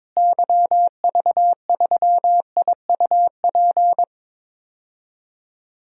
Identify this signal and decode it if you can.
Morse code, heard as a keyed tone, about 22 wpm, 700 Hz. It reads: YV3IUP
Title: DXCC-100 Call Signs - 6 Characters - Repeated 3x - 22wpm